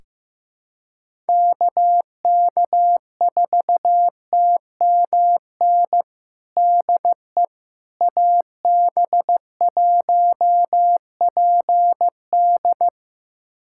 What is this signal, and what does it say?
Morse code, heard as a keyed tone, about 15 wpm, 700 Hz. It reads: KK4TMN DE AB1PD
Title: USA Callsign DE Callsign 15wpm